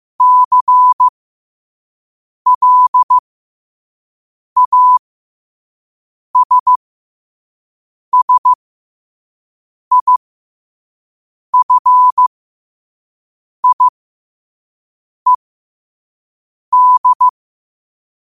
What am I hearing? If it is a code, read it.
Morse code, heard as a keyed tone, about 15 wpm, spaced as for 5 wpm, 1000 Hz. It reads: CLASSIFIED